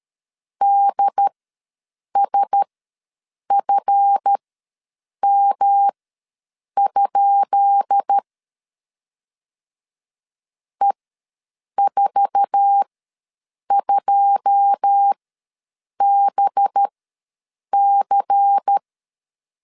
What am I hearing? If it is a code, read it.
Morse code, heard as a keyed tone, about 13 wpm, spaced as for 7 wpm, 800 Hz. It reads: DSFM? E42BC
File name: LEZIONI / 10 SIGNS / 10B INT ws 12wpm 3527